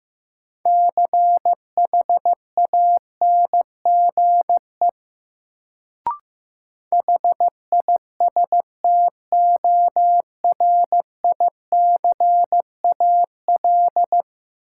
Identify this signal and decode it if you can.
Morse code, heard as a keyed tone, about 15 wpm, 700 Hz. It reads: CHANGE HISTORICAL
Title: Sentences from Top 600 Words - New Word Drill - 15wpm